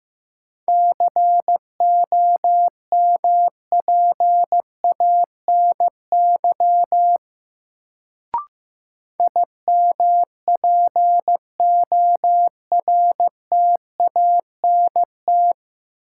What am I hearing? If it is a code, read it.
Morse code, heard as a keyed tone, about 15 wpm, 700 Hz. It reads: COMPANY IMPORTANT
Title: Sentences from Top 600 Words - New Word Drill - 15wpm